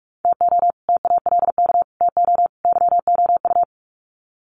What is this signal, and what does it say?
Morse code, heard as a keyed tone, about 44 wpm, 700 Hz. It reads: TO TALK TO YOU
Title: Sets of 4 Words 45wpm